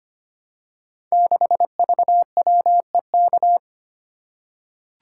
Morse code, heard as a keyed tone, about 25 wpm, 700 Hz. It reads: 6VWEK
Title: Code Groups - 5 Letter:Numbers - 25wpm